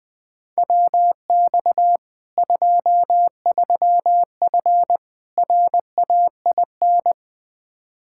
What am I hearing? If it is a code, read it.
Morse code, heard as a keyed tone, about 20 wpm, 700 Hz. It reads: WX 23F RAIN
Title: Practice QSO - Exchange 2 - Info - 20wpm